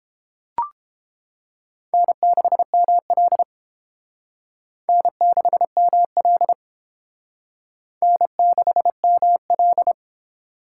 Morse code, heard as a keyed tone, about 28 wpm, 700 Hz. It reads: N6ML N6ML N6ML